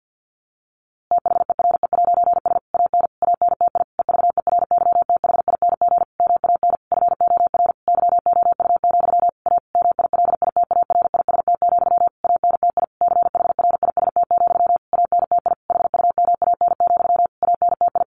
Morse code, heard as a keyed tone, about 50 wpm, 700 Hz. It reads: THERE'S AN ANTI EVERYTHING MAN FOR YOU, A MILITARIST, ANTI CHRIST, ANTI HUMAN, ANTI
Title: A Sound Of Thunder 50wpm